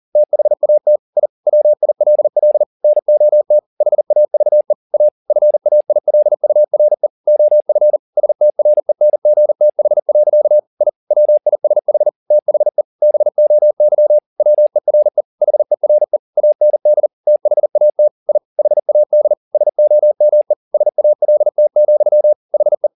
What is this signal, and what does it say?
Morse code, heard as a keyed tone, about 40 wpm, 600 Hz. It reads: THAT I WILL NOT HAVE A FAILURE OF STRENGTH. I WISH THE BOY WERE HERE AND THAT I HAD SOME SALT, HE